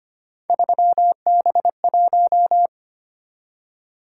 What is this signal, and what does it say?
Morse code, heard as a keyed tone, about 25 wpm, 700 Hz. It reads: 3B1
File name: Three Letter-Number Combinations - 25wpm